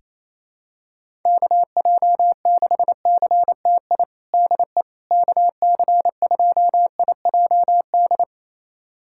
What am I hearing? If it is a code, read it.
Morse code, heard as a keyed tone, about 27 wpm, 700 Hz. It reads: KJ6CTI DE KC2IJD
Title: USA Callsign DE Callsign - 6 Characters - 28wpm